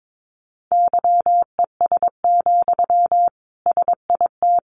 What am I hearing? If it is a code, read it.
Morse code, heard as a keyed tone, about 22 wpm, 700 Hz. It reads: YES, SIT
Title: Sentences from Top 500 Words 22wpm